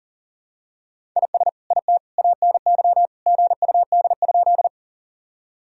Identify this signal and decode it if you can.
Morse code, heard as a keyed tone, about 40 wpm, 700 Hz. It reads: IS IT ANY GUD?
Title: Sentences from Top 200 Words and Abbreviations 40wpm